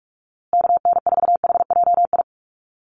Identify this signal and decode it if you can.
Morse code, heard as a keyed tone, about 44 wpm, 700 Hz. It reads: KN4HJI
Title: USA Call Signs Repeated 2x 45wpm